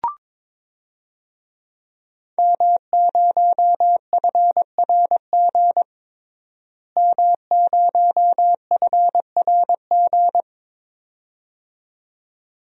Morse code, heard as a keyed tone, about 21 wpm, 700 Hz. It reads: M0FRG M0FRG
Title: DXCC-100 Call Signs - 5 Characters - Repeated 2x - 22wpm